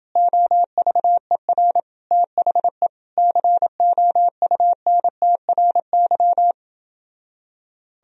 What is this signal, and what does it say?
Morse code, heard as a keyed tone, about 30 wpm, 700 Hz. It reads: OVER THE COUNTRY